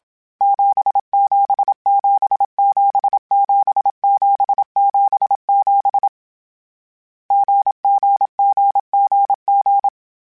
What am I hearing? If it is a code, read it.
Morse code, heard as a keyed tone, about 27 wpm, 800 Hz. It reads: 77777777 GGGGG
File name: Difficult Combos / Difficult Combos-mp3s / GZ GZ7 @25